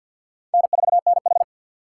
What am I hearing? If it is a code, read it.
Morse code, heard as a keyed tone, about 49 wpm, 700 Hz. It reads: N4NH
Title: USA Call Signs - 6x Character Spacing - 50wpm